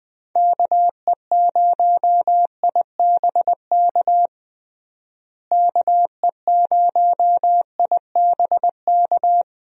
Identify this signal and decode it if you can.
Morse code, heard as a keyed tone, about 20 wpm, 700 Hz. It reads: KE0IBK KE0IBK
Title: USA Call Signs - 6 Characters - Repeated 2x - 20wpm